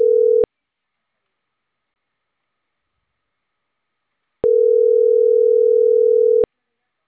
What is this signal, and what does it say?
Ringback tone (North America), 440 + 480 Hz.